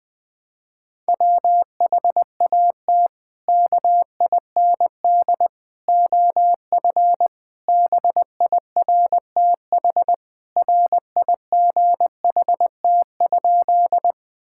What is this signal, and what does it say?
Morse code, heard as a keyed tone, about 21 wpm, 700 Hz. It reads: WHAT KIND OF BIRTHRIGHT?
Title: Sentences from Top 1000 Words - Review Part 5 - 20wpm